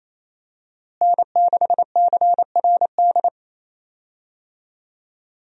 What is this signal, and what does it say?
Morse code, heard as a keyed tone, about 28 wpm, 700 Hz. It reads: N6CRD
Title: USA Call Signs - 5 Characters - Repeated 3x - 28wpm